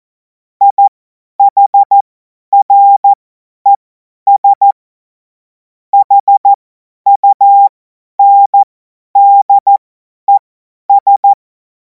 Morse code, heard as a keyed tone, about 14 wpm, spaced as for 10 wpm, 800 Hz. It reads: IHRES HUNDES
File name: 50 F70